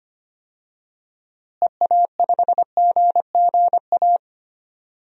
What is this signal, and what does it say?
Morse code, heard as a keyed tone, about 25 wpm, 700 Hz. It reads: EA5GGA
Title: DXCC-100 Call Signs - 6 Characters - Repeated 3x - 25wpm